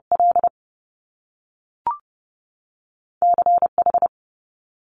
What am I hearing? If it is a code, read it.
Morse code, heard as a keyed tone, about 30 wpm, 700 Hz. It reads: L C H
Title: Top Two Letter Combinations in Words 30wpm